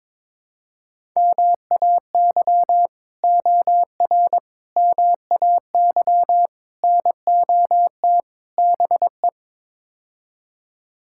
Morse code, heard as a keyed tone, about 22 wpm, 700 Hz. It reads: MAY OR MAY NOT BE
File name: practice / 22wpm / Build A Phrase - 5-Word Phrases - 22wpm